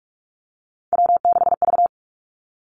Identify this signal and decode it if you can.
Morse code, heard as a keyed tone, about 46 wpm, 700 Hz. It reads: W6V
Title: Three Challenging Letter-Number Combinations 45wpm